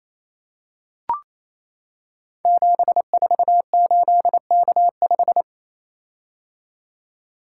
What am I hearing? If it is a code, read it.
Morse code, heard as a keyed tone, about 28 wpm, 700 Hz. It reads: 748K5
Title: Code Groups - 5 Letter:Numbers - 28wpm